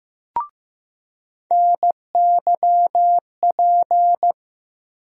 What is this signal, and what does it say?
Morse code, heard as a keyed tone, about 15 wpm, 700 Hz. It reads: NYP
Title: USA Call Sign Postfixes - 3 Letters 15wpm